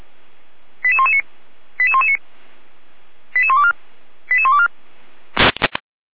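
Five-tone selective call (ZVEI), heard as quick runs of five tones, tones of about 70 ms. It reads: 88119 88119 88124 88124
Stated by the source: ZVEI five-tone selective call received with an SDR, on the audio server of Feuerwehr Orlamünde (fire brigade).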